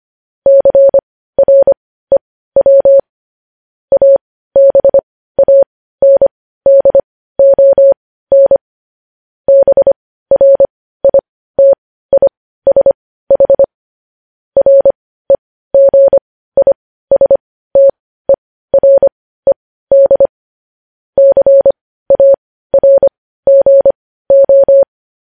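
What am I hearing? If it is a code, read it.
Morse code, heard as a keyed tone, about 25 wpm, spaced as for 15 wpm, 550 Hz. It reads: CREW ABANDON BRITISH REGISTERED CARGO
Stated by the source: Morse code news headlines practice at 15, preceded by vvv.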